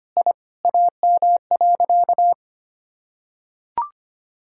Morse code, heard as a keyed tone, about 25 wpm, 700 Hz. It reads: I AM.
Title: Sentences from Top 1000 Words - Review Part 6 - 25wpm